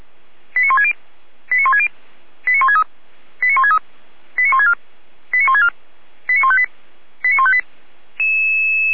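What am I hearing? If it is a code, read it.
Five-tone selective call (ZVEI), heard as quick runs of five tones, tones of about 70 ms. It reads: 89160 89160 89163 89163 89164 89164 89158 89158
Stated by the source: ZVEI five-tone selective call received by SDR.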